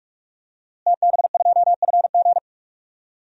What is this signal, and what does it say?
Morse code, heard as a keyed tone, about 45 wpm, 700 Hz. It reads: TB2FG
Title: DXCC-100 Call Signs - 5 Characters - Repeated 3x - 45wpm